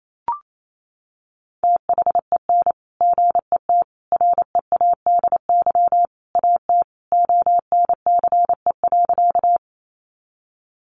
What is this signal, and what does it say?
Morse code, heard as a keyed tone, about 28 wpm, 700 Hz. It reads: THEN GET READY AT ONCE.
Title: Sentences from Top 400 Words 28wpm